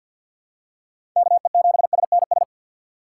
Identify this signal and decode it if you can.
Morse code, heard as a keyed tone, about 52 wpm, 700 Hz. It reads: KE7SNS
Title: USA Call Signs - 6 Characters - 3x Character Spacing - 50wpm